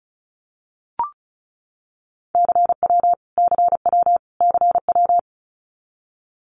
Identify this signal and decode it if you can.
Morse code, heard as a keyed tone, about 34 wpm, 700 Hz. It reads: CW CW CW
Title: Abbreviations Repeated 3x - 35wpm